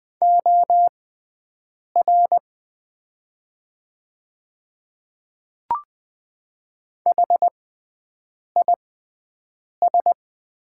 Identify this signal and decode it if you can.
Morse code, heard as a keyed tone, about 20 wpm, spaced as for 7 wpm, 700 Hz. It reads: OR HIS
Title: 6x Character Spacing - Top 100 Words - 20wpm